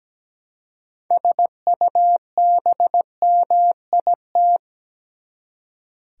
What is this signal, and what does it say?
Morse code, heard as a keyed tone, about 17 wpm, 700 Hz. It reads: SUBMIT
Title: Six Letter Words - 17wpm